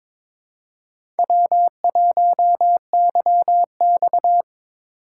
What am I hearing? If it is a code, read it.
Morse code, heard as a keyed tone, about 26 wpm, 700 Hz. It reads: W1YX
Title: Speed Racing - USA 1x2 Call Signs 22wpm